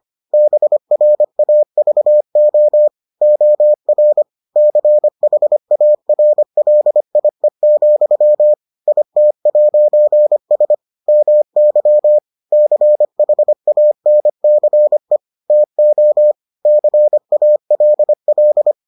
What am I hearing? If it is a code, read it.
Morse code, heard as a keyed tone, about 25 wpm, 600 Hz. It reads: BRAVO OR CHARLIE, IT'S MY CHANCE TO CALL